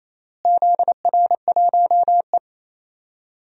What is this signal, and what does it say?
Morse code, heard as a keyed tone, about 27 wpm, 700 Hz. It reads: ZR1E
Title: DXCC-100 Call Signs - 4 Characters - Repeated 3x - 28wpm